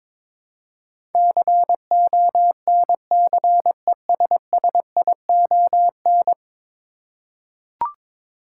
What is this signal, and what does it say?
Morse code, heard as a keyed tone, about 22 wpm, 700 Hz. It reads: CONCESSION
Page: morse-code-ninja-practice-22wpm